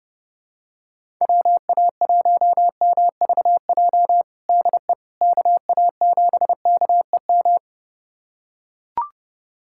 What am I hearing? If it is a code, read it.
Morse code, heard as a keyed tone, about 31 wpm, 700 Hz. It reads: WA1MVJ DE KA7KEM
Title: USA Callsign DE Callsign - 6 Characters - 30wpm